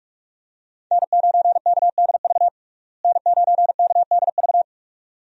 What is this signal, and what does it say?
Morse code, heard as a keyed tone, about 46 wpm, 700 Hz. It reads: N9KDV N9KDV
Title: USA Call Signs - 5 Characters - Repeated 2x - 45wpm